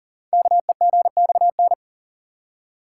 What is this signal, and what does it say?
Morse code, heard as a keyed tone, about 42 wpm, 700 Hz. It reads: KEGXN
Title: Code Groups - 5 Letters - 40wpm